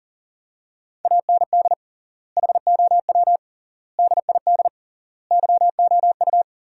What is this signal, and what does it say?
Morse code, heard as a keyed tone, about 37 wpm, 700 Hz. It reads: AND HOW DID YOU
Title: 3x Word-Spacing - Sentences from Top 100 Words - 40wpm